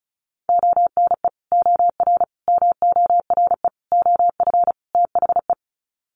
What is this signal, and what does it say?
Morse code, heard as a keyed tone, about 34 wpm, 700 Hz. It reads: ONE OR MORE OF THE
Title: Sets of 5 Words Encore 35wpm